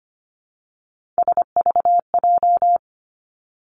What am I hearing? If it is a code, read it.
Morse code, heard as a keyed tone, about 25 wpm, 700 Hz. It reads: SVJ
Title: Three Challenging Letter-Number Combinations 25wpm